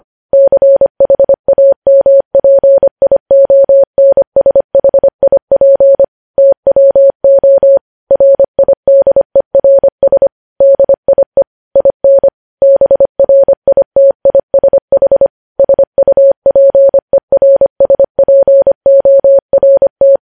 Morse code, heard as a keyed tone, about 25 wpm, 550 Hz. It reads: CHAMPIONSHIP TWO RIDERS DIE IN BRITISH SUPERSPORT
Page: news headlines in morse code at 25 wpm